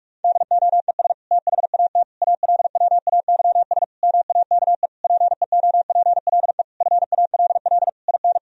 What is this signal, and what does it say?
Morse code, heard as a keyed tone, about 42 wpm, 700 Hz. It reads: DOES THAT ALWAYS MAKE PEOPLE FALL IN
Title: Sentences from Top 1000 Words - Review Part 2 - 45wpm